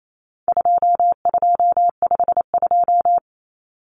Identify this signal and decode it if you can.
Morse code, heard as a keyed tone, about 28 wpm, 700 Hz. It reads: 2252
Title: Common Cut Numbers - 4-Digit 28wpm